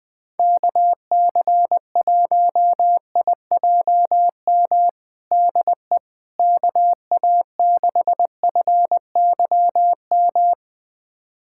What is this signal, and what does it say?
Morse code, heard as a keyed tone, about 21 wpm, 700 Hz. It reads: KC1IJM DE KA6FYM
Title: USA Callsign DE Callsign - 6 Characters - 20wpm